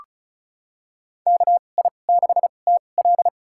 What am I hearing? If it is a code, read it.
Morse code, heard as a keyed tone, about 34 wpm, spaced as for 25 wpm, 700 Hz. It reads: KI6TL
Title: USA Call Signs - 5 Characters - 2x Character Spacing - 35wpm